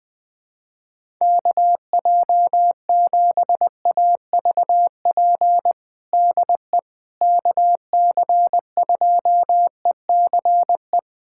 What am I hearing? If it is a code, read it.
Morse code, heard as a keyed tone, about 21 wpm, 700 Hz. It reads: KJ7AVP DE KC2ECE
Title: USA Callsign DE Callsign 20wpm